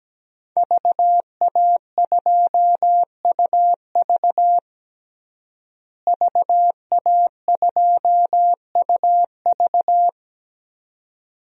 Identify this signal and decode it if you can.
Morse code, heard as a keyed tone, about 17 wpm, 700 Hz. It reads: VA2UV VA2UV
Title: DXCC-100 Call Signs - 5 Characters - Repeated 2x - 17wpm